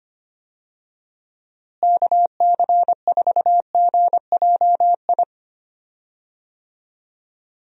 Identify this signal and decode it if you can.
Morse code, heard as a keyed tone, about 25 wpm, 700 Hz. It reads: KC4GJI
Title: USA Call Signs - 6 Characters - Repeated 2x - 25wpm